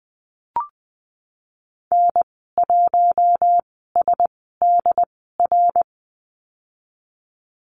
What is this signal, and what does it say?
Morse code, heard as a keyed tone, about 20 wpm, spaced as for 15 wpm, 700 Hz. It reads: N1SDR
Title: 076 Lesson 13g - Letter L - Call Signs 20wpm